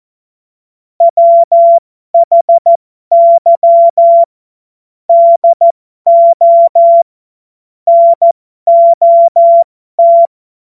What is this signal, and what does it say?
Morse code, heard as a keyed tone, about 14 wpm, spaced as for 12 wpm, 680 Hz. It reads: WHY DO NOT
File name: Morsekurs / Witze